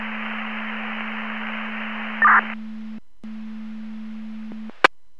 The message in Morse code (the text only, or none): none